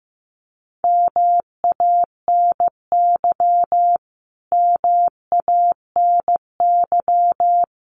Morse code, heard as a keyed tone, about 15 wpm, 700 Hz. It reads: MANY MANY
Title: Top 500 Words Repeated 15wpm